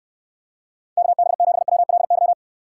Morse code, heard as a keyed tone, about 67 wpm, 700 Hz. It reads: KB7WLY